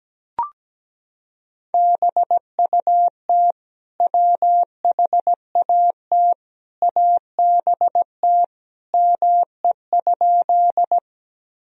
Morse code, heard as a keyed tone, about 17 wpm, 700 Hz. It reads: BUT WHAT ABT ME?
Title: Sentences from Top 200 Words and Abbreviations 17wpm